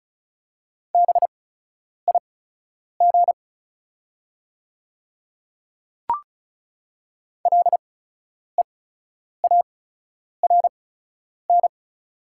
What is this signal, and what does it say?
Morse code, heard as a keyed tone, about 36 wpm, spaced as for 10 wpm, 700 Hz. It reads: BIG LEARN
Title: 8x Character Spacing - Top 300 Words - 35wpm